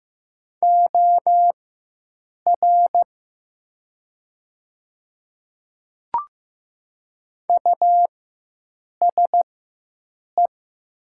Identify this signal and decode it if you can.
Morse code, heard as a keyed tone, about 15 wpm, spaced as for 7 wpm, 700 Hz. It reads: OR USE
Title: 4x Character Spacing - Top 200 Words - 15wpm